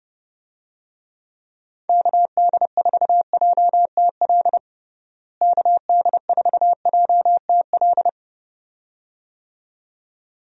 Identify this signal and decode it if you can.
Morse code, heard as a keyed tone, about 30 wpm, 700 Hz. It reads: KD4JTL KD4JTL
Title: USA Call Signs - 6 Characters - Repeated 2x - 30wpm